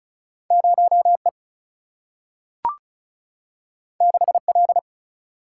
Morse code, heard as a keyed tone, about 36 wpm, 700 Hz. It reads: 0E 6L
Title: Single-Letter Single-Number 35wpm